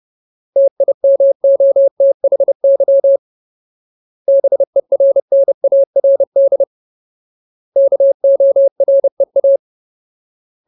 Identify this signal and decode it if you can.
Morse code, heard as a keyed tone, about 30 wpm, 550 Hz. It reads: TIMOTHY BERNARD KOREA
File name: Ultimate Word List / 30WPM Ultimate